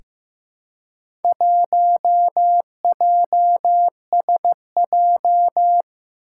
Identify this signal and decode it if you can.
Morse code, heard as a keyed tone, about 15 wpm, 700 Hz. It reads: 1JSJ
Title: Four Challenging Letter-Number Combinations 15wpm